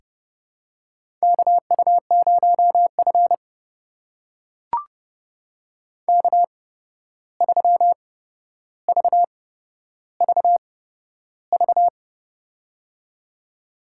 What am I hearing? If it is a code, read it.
Morse code, heard as a keyed tone, about 28 wpm, spaced as for 12 wpm, 700 Hz. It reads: KU0F K3VVV
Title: USA Call Signs - 8x Character Spacing - 30wpm